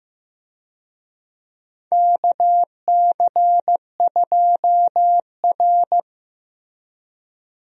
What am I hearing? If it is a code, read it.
Morse code, heard as a keyed tone, about 16 wpm, 700 Hz. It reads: KC2R
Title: USA Call Signs - 4 Characters - Repeated 2x - 15wpm